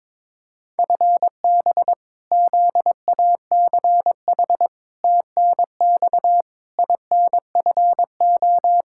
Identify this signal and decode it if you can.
Morse code, heard as a keyed tone, about 22 wpm, 700 Hz. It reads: FB ZACH TNX INFO
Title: K1USN SST QSO Practice 22wpm